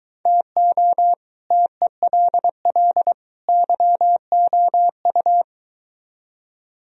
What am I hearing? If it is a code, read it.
Morse code, heard as a keyed tone, about 23 wpm, 700 Hz. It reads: TO TELL YOU